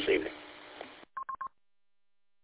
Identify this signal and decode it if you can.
Speech. There is no code.